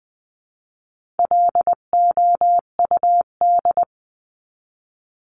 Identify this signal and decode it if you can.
Morse code, heard as a keyed tone, about 20 wpm, 700 Hz. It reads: LOUD
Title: 259 Lesson 42d - Words 18-20wpm